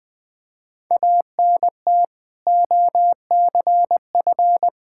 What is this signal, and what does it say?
Morse code, heard as a keyed tone, about 20 wpm, 700 Hz. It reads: ANT OCF
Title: Practice QSO - Exchange 2 - Info - 20wpm